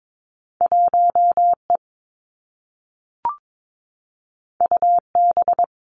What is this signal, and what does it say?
Morse code, heard as a keyed tone, about 22 wpm, 700 Hz. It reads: 1 E U B
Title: Two Challenging Letter-Number Combinations 22wpm